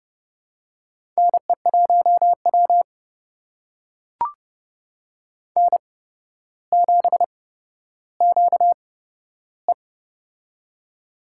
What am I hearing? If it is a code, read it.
Morse code, heard as a keyed tone, about 29 wpm, spaced as for 13 wpm, 700 Hz. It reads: NE1W N7QE
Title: USA Call Signs - 4 Characters - 8x Character Spacing - 30wpm